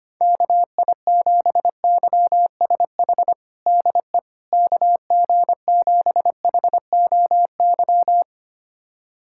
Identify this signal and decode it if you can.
Morse code, heard as a keyed tone, about 25 wpm, 700 Hz. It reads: KI7YSH DE KG7HOY
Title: USA Callsign DE Callsign 25wpm